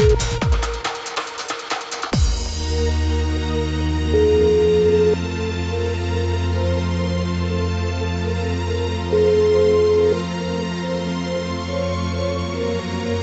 Ringback tone (Europe, 425 Hz), with noise.